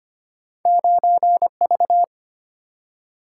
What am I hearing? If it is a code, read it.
Morse code, heard as a keyed tone, about 25 wpm, 700 Hz. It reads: 9V
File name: Two Letter-Number Combinations - 25wpm